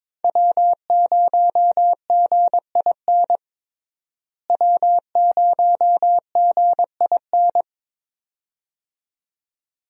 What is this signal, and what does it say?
Morse code, heard as a keyed tone, about 23 wpm, 700 Hz. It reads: W0GIN W0GIN
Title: USA Call Signs - 5 Characters - Repeated 2x - 22wpm